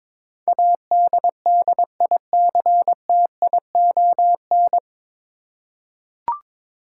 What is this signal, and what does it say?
Morse code, heard as a keyed tone, about 22 wpm, 700 Hz. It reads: ADDICTION
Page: Morse Code Ninja practice 22wpm